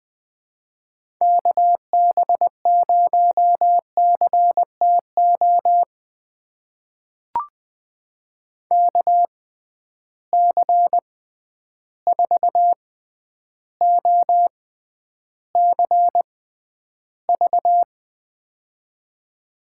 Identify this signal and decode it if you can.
Morse code, heard as a keyed tone, about 21 wpm, spaced as for 10 wpm, 700 Hz. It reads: KB0CTO KC4OCV
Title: USA Call Signs - 6 Characters - 6x Character Spacing - 20wpm